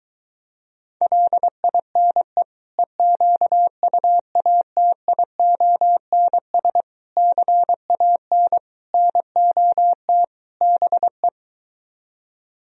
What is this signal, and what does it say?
Morse code, heard as a keyed tone, about 24 wpm, 700 Hz. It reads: LINE EQUATIONS CAN NOT BE